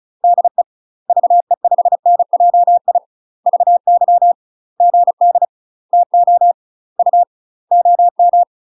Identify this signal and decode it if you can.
Morse code, heard as a keyed tone, about 34 wpm, 700 Hz. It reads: DE VE5NJI VY GD TO U OM
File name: UK MASTER QSOs TRAINING/35WPM